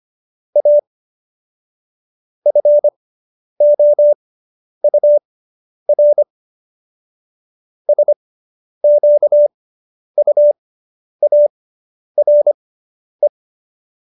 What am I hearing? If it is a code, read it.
Morse code, heard as a keyed tone, about 25 wpm, spaced as for 10 wpm, 590 Hz. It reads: A FOUR SQUARE